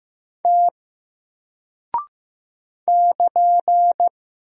Morse code, heard as a keyed tone, about 15 wpm, 700 Hz. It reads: T <KN>
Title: Single Character (Letter:Number:Punctuation:Prosign) Alt 15wpm